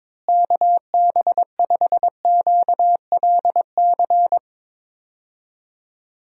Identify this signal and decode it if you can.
Morse code, heard as a keyed tone, about 22 wpm, 700 Hz. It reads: KB5QLC
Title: USA Call Signs - 6 Characters - Repeated 2x - 22wpm